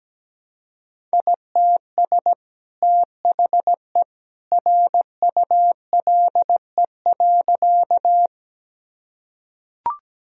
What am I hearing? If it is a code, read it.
Morse code, heard as a keyed tone, about 17 wpm, 700 Hz. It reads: ITS THE RULE.
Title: Sentences from Top 1000 Words - Review Part 2 - 17wpm